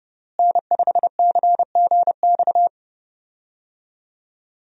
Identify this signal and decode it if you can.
Morse code, heard as a keyed tone, about 30 wpm, 700 Hz. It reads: N5CGX
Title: USA Call Signs - 5 Characters - Repeated 3x - 30wpm